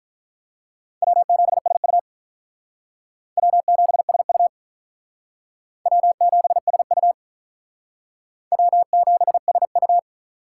Morse code, heard as a keyed tone, about 45 wpm, 700 Hz. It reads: W7SU W7SU W7SU W7SU